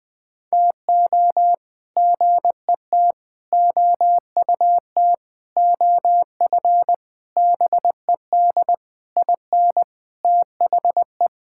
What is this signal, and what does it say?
Morse code, heard as a keyed tone, about 20 wpm, 700 Hz. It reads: TO GET OUT OF BED IN THE
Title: Sentences with 8-Word Phrases - Spoken Beginning - 20wpm